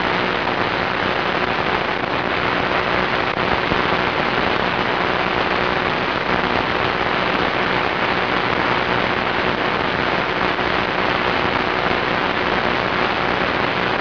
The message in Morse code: none